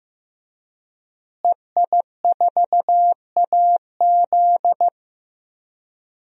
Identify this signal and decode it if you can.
Morse code, heard as a keyed tone, about 15 wpm, 700 Hz. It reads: EI4AZ
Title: DXCC-100 Call Signs - 5 Characters - Repeated 2x - 15wpm